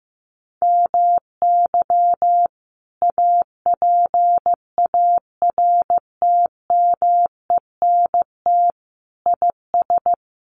Morse code, heard as a keyed tone, about 15 wpm, 700 Hz. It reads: MY APARTMENT IS
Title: Sentences from Top 900 Words 15wpm